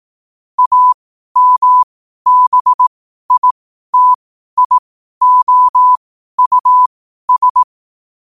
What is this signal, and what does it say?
Morse code, heard as a keyed tone, about 18 wpm, spaced as for 12 wpm, 1000 Hz. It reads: AMBITIOUS
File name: qotdfiles / 12wpm